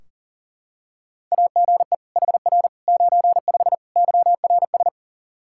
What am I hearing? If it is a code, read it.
Morse code, heard as a keyed tone, about 40 wpm, 700 Hz. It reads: AGE HR 95 YRS
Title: Practice QSO - Exchange 3 - More Info - 40wpm